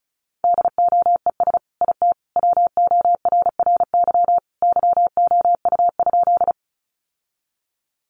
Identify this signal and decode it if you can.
Morse code, heard as a keyed tone, about 34 wpm, 700 Hz. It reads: DOES IT WORRY YOU?
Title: Sentences from Top 1100 Words 35wpm